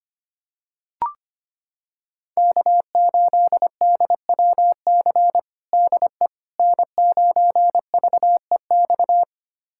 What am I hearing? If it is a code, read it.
Morse code, heard as a keyed tone, about 25 wpm, 700 Hz. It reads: K8DWC DE N9VEX
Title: USA Callsign DE Callsign - 5 Characters - 25wpm